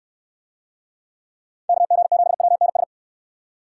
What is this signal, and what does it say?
Morse code, heard as a keyed tone, about 69 wpm, 700 Hz. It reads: KG7WNS